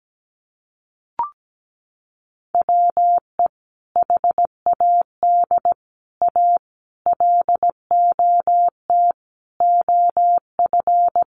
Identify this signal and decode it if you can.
Morse code, heard as a keyed tone, about 17 wpm, 700 Hz. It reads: WE HAD A LOT OF
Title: Sets of 5 Words Encore 17wpm